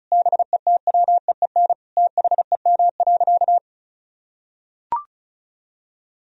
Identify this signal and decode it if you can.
Morse code, heard as a keyed tone, about 34 wpm, 700 Hz. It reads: BETWEEN THEM.
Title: Sentences from Top 1000 Words - Review Part 2 - 35wpm